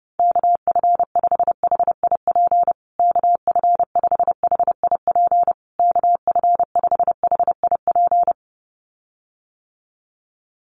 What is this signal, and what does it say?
Morse code, heard as a keyed tone, about 28 wpm, 700 Hz. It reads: KF5HIP KF5HIP KF5HIP
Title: USA Call Signs Repeated 3x 30wpm